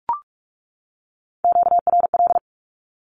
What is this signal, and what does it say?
Morse code, heard as a keyed tone, about 44 wpm, 700 Hz. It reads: QRL
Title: Q-Codes 45wpm